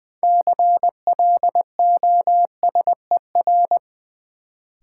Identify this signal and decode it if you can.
Morse code, heard as a keyed tone, about 20 wpm, 700 Hz. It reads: CLOSER
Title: Sets of 5 Words Encore - New Word Drill - 20wpm